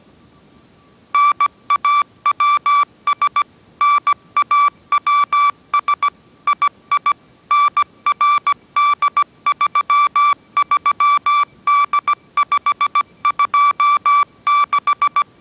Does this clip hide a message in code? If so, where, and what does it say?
Yes: Morse code from 1.1 to 15.2 s, heard as a keyed tone, about 17 wpm, 1160 Hz. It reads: NAWS NAWS II NRD33D526